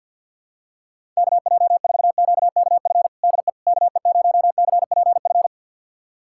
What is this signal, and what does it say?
Morse code, heard as a keyed tone, about 53 wpm, 700 Hz. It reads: KJ4QKF DE KE0CPF
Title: USA Callsign DE Callsign - 6 Characters - 50wpm